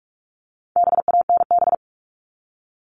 Morse code, heard as a keyed tone, about 44 wpm, 700 Hz. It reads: BANB
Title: All Cut Numbers - 4-Digit 45wpm